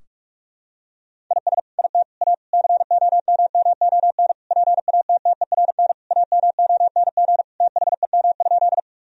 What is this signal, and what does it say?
Morse code, heard as a keyed tone, about 42 wpm, 700 Hz. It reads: IS IT A COMMON PATTERN AMONG THEM?
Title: Sentences from Top 1000 Words - Review Part 2 - 45wpm